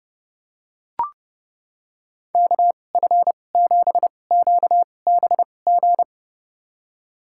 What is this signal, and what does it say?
Morse code, heard as a keyed tone, about 32 wpm, spaced as for 22 wpm, 700 Hz. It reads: KF7QBG
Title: USA Call Signs - 6 Characters - 2x Character Spacing - 30wpm